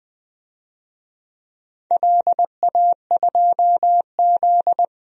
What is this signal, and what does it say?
Morse code, heard as a keyed tone, about 21 wpm, 700 Hz. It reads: LA2Z
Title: DXCC-100 Call Signs - 4 Characters - 20wpm